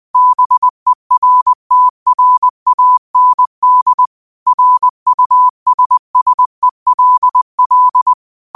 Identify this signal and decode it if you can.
Morse code, heard as a keyed tone, about 20 wpm, 1000 Hz. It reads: BERTRAND RUSSELL